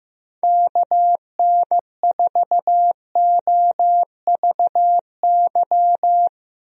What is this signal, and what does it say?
Morse code, heard as a keyed tone, about 16 wpm, 700 Hz. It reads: KN4OVY
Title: USA Call Signs - 6 Characters - Repeated 3x - 15wpm